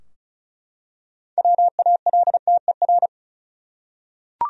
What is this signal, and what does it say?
Morse code, heard as a keyed tone, about 34 wpm, 700 Hz. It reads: WALTER
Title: USA Common Names - 35wpm